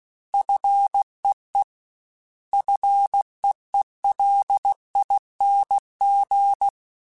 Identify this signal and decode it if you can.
Morse code, heard as a keyed tone, about 16 wpm, 800 Hz. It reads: FEE FEELING
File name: Combinations / Combinations-mp3s / Combinations@15